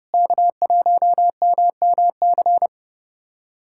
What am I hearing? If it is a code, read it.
Morse code, heard as a keyed tone, about 30 wpm, 700 Hz. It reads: K1MMC
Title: USA Callsign DE Callsign - 5 Characters - 30wpm